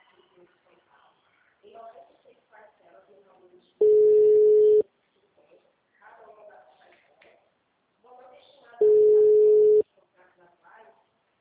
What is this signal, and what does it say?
Ringback tone (Europe, 425 Hz), with speech.
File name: out- -2011- - - .13319